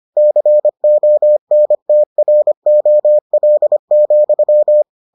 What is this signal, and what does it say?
Morse code, heard as a keyed tone, about 25 wpm, 600 Hz. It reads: CONTROL,